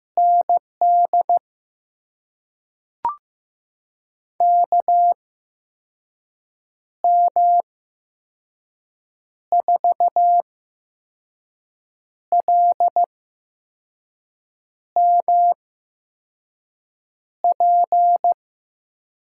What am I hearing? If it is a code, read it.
Morse code, heard as a keyed tone, about 15 wpm, spaced as for 5 wpm, 700 Hz. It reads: ND KM4LMP
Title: USA Call Signs - 6 Characters - 8x Character Spacing - 15wpm